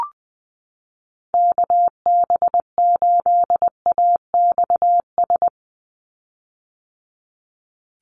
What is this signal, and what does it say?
Morse code, heard as a keyed tone, about 20 wpm, 700 Hz. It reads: KB8AXS
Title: USA Call Signs 20wpm